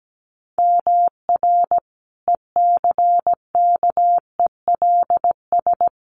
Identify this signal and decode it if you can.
Morse code, heard as a keyed tone, about 17 wpm, 700 Hz. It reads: MR ECKELS
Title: A Sound Of Thunder 17wpm